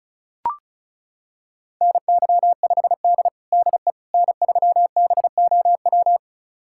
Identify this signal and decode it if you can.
Morse code, heard as a keyed tone, about 36 wpm, 700 Hz. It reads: NY5D DE N3BOW
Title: USA Callsign DE Callsign 35wpm